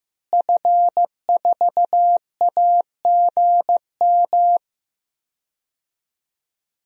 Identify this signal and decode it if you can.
Morse code, heard as a keyed tone, about 16 wpm, 700 Hz. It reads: F4AGM
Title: DXCC-100 Call Signs - 5 Characters - Repeated 3x - 15wpm